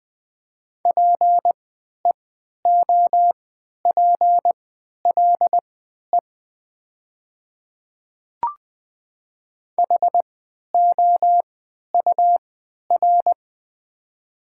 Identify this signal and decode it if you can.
Morse code, heard as a keyed tone, about 20 wpm, spaced as for 11 wpm, 700 Hz. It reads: PEOPLE HOUR
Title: 3x Character Spacing - Top 500 Words - 20wpm